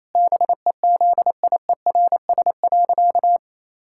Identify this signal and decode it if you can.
Morse code, heard as a keyed tone, about 27 wpm, 700 Hz. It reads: BEZIERS.
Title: Pangram Challenge v2 - 28wpm